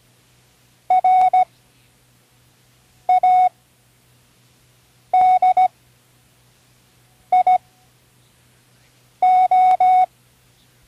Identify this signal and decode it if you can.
Morse code, heard as a keyed tone, about 17 wpm, spaced as for 5 wpm, 740 Hz. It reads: RADIO
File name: CW Learning Programs / Learning Morse Code (from Tune in the World) / 08 words